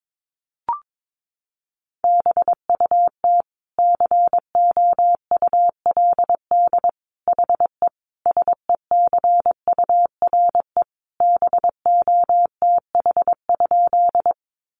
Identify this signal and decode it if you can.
Morse code, heard as a keyed tone, about 22 wpm, 700 Hz. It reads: BUT COULD HE SECURE BOTH?
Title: Sentences from Top 1200 Words 22wpm